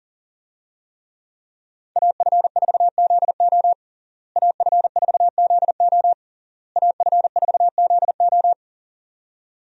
Morse code, heard as a keyed tone, about 40 wpm, 700 Hz. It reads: AF4ZO AF4ZO AF4ZO
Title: USA Call Signs - 5 Characters - Repeated 3x - 40wpm